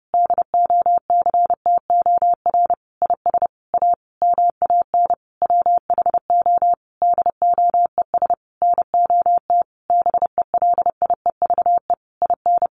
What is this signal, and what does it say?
Morse code, heard as a keyed tone, about 30 wpm, 700 Hz. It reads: DOCTOR IS A MAN WHO DOES NOT BELIEVE IN
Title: Sentences from Top 1500 Words 30wpm